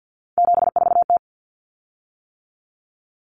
Morse code, heard as a keyed tone, about 48 wpm, 700 Hz. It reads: 74T
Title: Common Cut Numbers - 3-Digit 50wpm